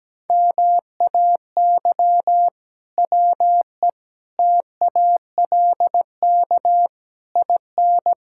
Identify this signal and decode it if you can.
Morse code, heard as a keyed tone, about 17 wpm, 700 Hz. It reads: MAY WE TALK IN
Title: Sentences from Top 1600 Words 17wpm